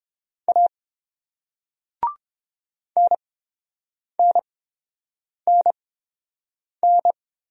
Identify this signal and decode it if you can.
Morse code, heard as a keyed tone, about 29 wpm, spaced as for 7 wpm, 700 Hz. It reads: A NNNN